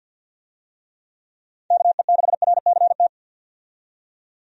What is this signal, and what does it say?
Morse code, heard as a keyed tone, about 53 wpm, 700 Hz. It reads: KE6RCT
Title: USA Call Signs - 6 Characters - 50wpm